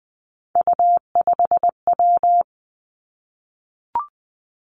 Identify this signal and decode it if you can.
Morse code, heard as a keyed tone, about 20 wpm, 700 Hz. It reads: U5W
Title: Three Challenging Letter-Number Combinations 20wpm